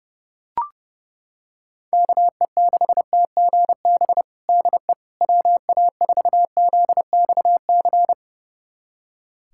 Morse code, heard as a keyed tone, about 32 wpm, 700 Hz. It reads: KE6TGB DE WA4ZXC
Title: USA Callsign DE Callsign 30wpm